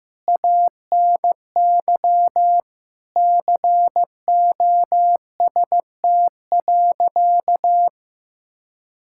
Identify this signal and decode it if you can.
Morse code, heard as a keyed tone, about 15 wpm, 700 Hz. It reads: ANY COST.
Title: Sentences from Top 1000 Words - Review Part 5 - 15wpm